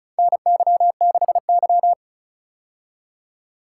Morse code, heard as a keyed tone, about 36 wpm, 700 Hz. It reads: NY6Y
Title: K1USN SST QSO Practice - 35wpm